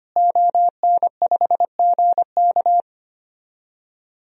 Morse code, heard as a keyed tone, about 25 wpm, 700 Hz. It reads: ON5GK
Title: DXCC-100 Call Signs - 5 Characters - Repeated 3x - 25wpm